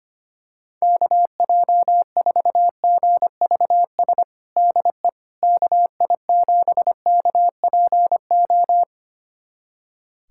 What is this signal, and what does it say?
Morse code, heard as a keyed tone, about 25 wpm, 700 Hz. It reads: KJ4GVS DE KI7KPO
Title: USA Callsign DE Callsign - 6 Characters - 25wpm